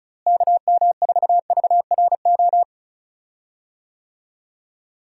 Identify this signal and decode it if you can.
Morse code, heard as a keyed tone, about 36 wpm, 700 Hz. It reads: KM4VRO
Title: USA Call Signs - 6 Characters - Repeated 3x - 35wpm